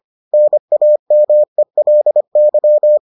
Morse code, heard as a keyed tone, about 25 wpm, 600 Hz. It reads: NAMELY